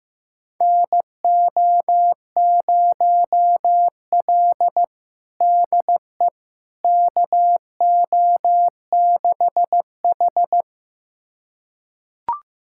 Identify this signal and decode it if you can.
Morse code, heard as a keyed tone, about 15 wpm, 700 Hz. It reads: NO0L DE KO6H
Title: USA Callsign DE Callsign - 4 Characters - 15wpm